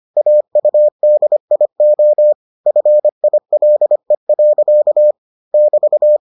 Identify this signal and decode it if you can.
Morse code, heard as a keyed tone, about 25 wpm, 600 Hz. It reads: AUDIO FILE. <BT>